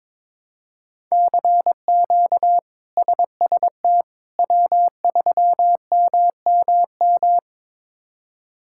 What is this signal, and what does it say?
Morse code, heard as a keyed tone, about 22 wpm, 700 Hz. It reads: CQ SST W3MMM
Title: K1USN SST QSO Practice - 22wpm